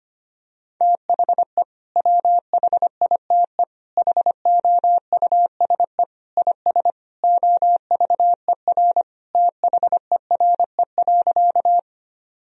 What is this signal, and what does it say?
Morse code, heard as a keyed tone, about 25 wpm, 700 Hz. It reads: THE WHITE HOUSE IS OVER THERE.
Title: Sentences from Top 1000 Words - Review Part 1 - 25wpm